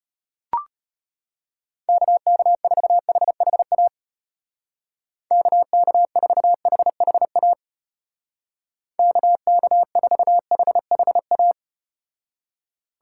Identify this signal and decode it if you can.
Morse code, heard as a keyed tone, about 33 wpm, 700 Hz. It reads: KK4HHA KK4HHA KK4HHA